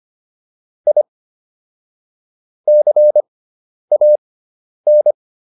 Morse code, heard as a keyed tone, about 25 wpm, spaced as for 10 wpm, 600 Hz. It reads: I CAN
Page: intermediate-practice-files